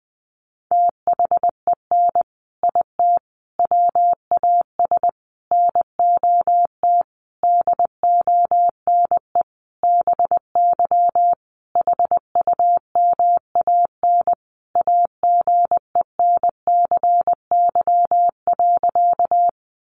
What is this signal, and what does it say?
Morse code, heard as a keyed tone, about 19 wpm, 700 Hz. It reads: THEN IT WAS NOT DONE BY HUMAN AGENCY.
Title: Sentences from Top 1300 Words 20wpm